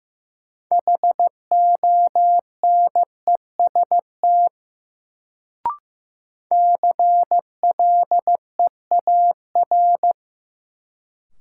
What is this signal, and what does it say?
Morse code, heard as a keyed tone, about 15 wpm, 700 Hz. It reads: HONEST CLEAR
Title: Sets of 5 Words Encore - New Word Drill - 15wpm